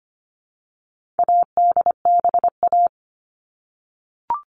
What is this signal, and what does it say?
Morse code, heard as a keyed tone, about 25 wpm, 700 Hz. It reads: ADBA
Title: All Cut Numbers - 4-Digit 25wpm